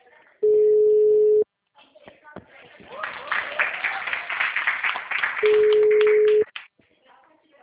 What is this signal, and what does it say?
Ringback tone (Europe, 425 Hz), with speech.